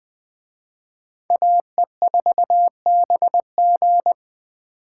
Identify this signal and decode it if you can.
Morse code, heard as a keyed tone, about 21 wpm, 700 Hz. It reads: AE4BG
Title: Speed Racing - USA 2x2 Call Signs 17wpm